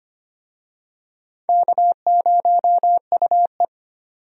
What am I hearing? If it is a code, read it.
Morse code, heard as a keyed tone, about 25 wpm, 700 Hz. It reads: K0UE
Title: USA Call Signs - 4 Characters - 25wpm